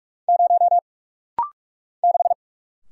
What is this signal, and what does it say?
Morse code, heard as a keyed tone, about 44 wpm, 700 Hz. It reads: 0 6
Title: Single Letter-Number Alt - Rapid-Fire 45wpm